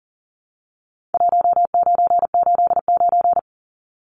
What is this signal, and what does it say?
Morse code, heard as a keyed tone, about 40 wpm, 700 Hz. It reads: 1989
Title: Numbers - 4-Digit 40wpm